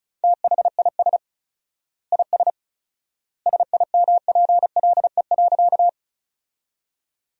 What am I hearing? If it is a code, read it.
Morse code, heard as a keyed tone, about 36 wpm, 700 Hz. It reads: THIS IS SIMPLE.
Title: 4x Word-Spacing - Sentences from Top 400 Words - 35wpm